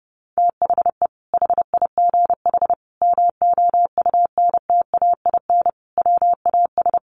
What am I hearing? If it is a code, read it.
Morse code, heard as a keyed tone, about 30 wpm, 700 Hz. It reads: THE HIGH MOUNTAIN WAS
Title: Sentences from Top 1300 Words 30wpm